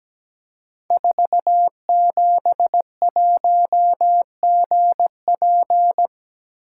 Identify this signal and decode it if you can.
Morse code, heard as a keyed tone, about 17 wpm, 700 Hz. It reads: 471GP
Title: Code Groups - 5 Letter:Numbers - 17wpm